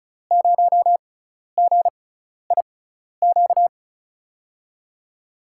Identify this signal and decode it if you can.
Morse code, heard as a keyed tone, about 34 wpm, spaced as for 12 wpm, 700 Hz. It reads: 0GIQ
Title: USA Call Signs - 6 Characters - 6x Character Spacing - 35wpm